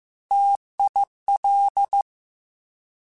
Morse code, heard as a keyed tone, about 15 wpm, 800 Hz. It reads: TIL
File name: T Prefixes / T mp3s